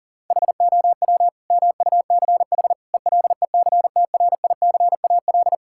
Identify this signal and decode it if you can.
Morse code, heard as a keyed tone, about 40 wpm, 700 Hz. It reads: HOW MUCH ELECTRICAL
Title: Sentences from Top 1000 Words - Review Part 6 - 40wpm